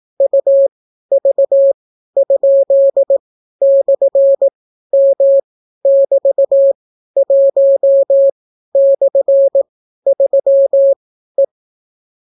Morse code, heard as a keyed tone, about 18 wpm, spaced as for 12 wpm, 550 Hz. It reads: UV?/M<BT>1/3E